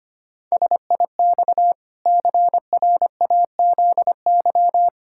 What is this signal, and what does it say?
Morse code, heard as a keyed tone, about 25 wpm, 700 Hz. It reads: SIX CRAZY
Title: Pangram Challenge v2 - 25wpm